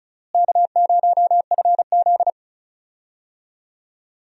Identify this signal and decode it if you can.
Morse code, heard as a keyed tone, about 36 wpm, 700 Hz. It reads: K0FZ
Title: USA Call Signs - 4 Characters - Repeated 2x - 35wpm